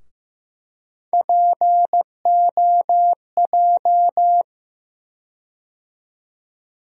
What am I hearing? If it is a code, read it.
Morse code, heard as a keyed tone, about 15 wpm, 700 Hz. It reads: POJ
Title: USA Call Sign Postfixes - 3 Letters 15wpm